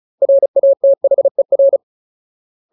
Morse code, heard as a keyed tone, about 34 wpm, 550 Hz. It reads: RATHER
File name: Ultimate Word List / 35WPM Ultimate